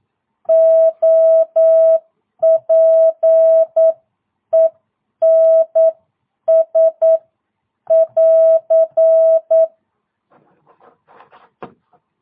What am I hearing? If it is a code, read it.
Morse code, heard as a keyed tone, about 9 wpm, spaced as for 8 wpm, 650 Hz. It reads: OPENS <AR>
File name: jh8cbh / 2022cw